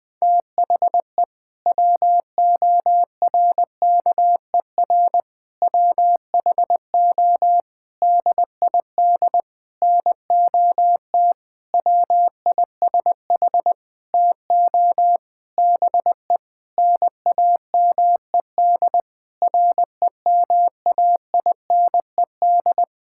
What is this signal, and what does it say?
Morse code, heard as a keyed tone, about 21 wpm, 700 Hz. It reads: THE WORKER WHO DID NOT WISH TO BE NAMED REMAINED
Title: Sentences with 7-Word Phrases - 20wpm